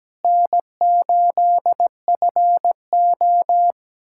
Morse code, heard as a keyed tone, about 17 wpm, 700 Hz. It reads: N8FO
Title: USA Callsign DE Callsign - 4 Characters - 17wpm